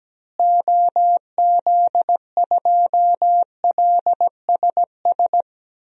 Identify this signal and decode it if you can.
Morse code, heard as a keyed tone, about 17 wpm, 700 Hz. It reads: OZ2LSS
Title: DXCC-100 Call Signs - 6 Characters - Repeated 3x - 17wpm